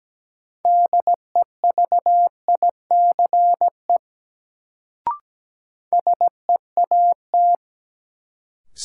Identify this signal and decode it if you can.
Morse code, heard as a keyed tone, about 17 wpm, 700 Hz. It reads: DEVICE SEAT
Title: Sentences from Top 1500 Words - New Word Drill Alt - 17wpm